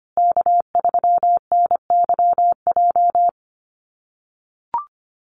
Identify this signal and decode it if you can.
Morse code, heard as a keyed tone, about 25 wpm, 700 Hz. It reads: K3NYJ
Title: USA Call Signs Repeated 3x 25wpm